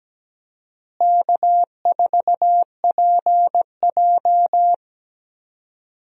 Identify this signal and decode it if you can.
Morse code, heard as a keyed tone, about 17 wpm, 700 Hz. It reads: K4PJ
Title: USA Call Signs - 4 Characters - Repeated 3x - 17wpm